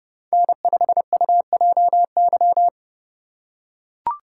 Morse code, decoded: N5UJY

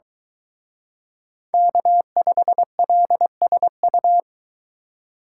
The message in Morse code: K5LSU

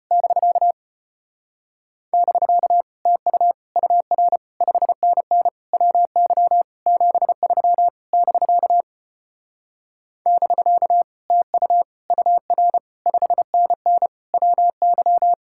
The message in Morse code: <BK> <BK> TU UR 5NN WY 73 <BK> <BK> TU UR 5NN WY

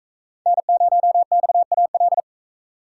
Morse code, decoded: N0XAL

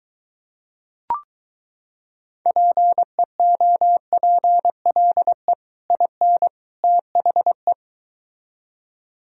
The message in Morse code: PEOPLE IN THE

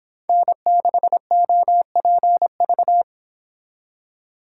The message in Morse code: N6OPV

